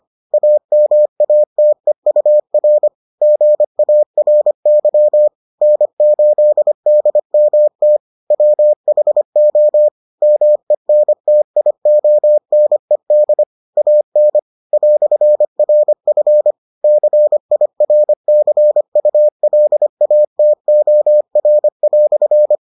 Morse code, heard as a keyed tone, about 25 wpm, 600 Hz. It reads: AMATEUR GARY N8DMT WHO MENTIONED AN "RF CIRCULATOR"